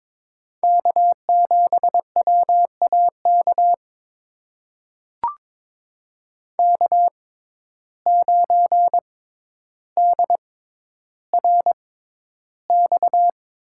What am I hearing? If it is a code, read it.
Morse code, heard as a keyed tone, about 21 wpm, spaced as for 11 wpm, 700 Hz. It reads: K7WAK K9DRX
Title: USA Call Signs - 5 Characters - 6x Character Spacing - 22wpm